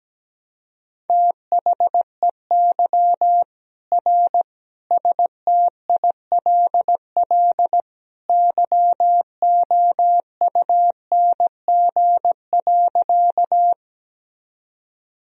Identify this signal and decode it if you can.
Morse code, heard as a keyed tone, about 17 wpm, 700 Hz. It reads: THEY R STILL YOUNG.
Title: Sentences from Top 400 Words and Abbreviations 17wpm